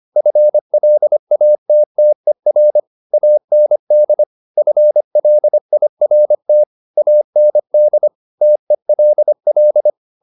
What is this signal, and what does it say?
Morse code, heard as a keyed tone, about 25 wpm, 600 Hz. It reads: FLATTER AND FLIRT AND TELL